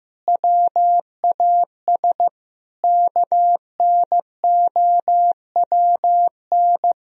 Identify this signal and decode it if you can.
Morse code, heard as a keyed tone, about 15 wpm, 700 Hz. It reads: WAS KNOWN
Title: Sentences from Top 1000 Words - Review Part 3 - 15wpm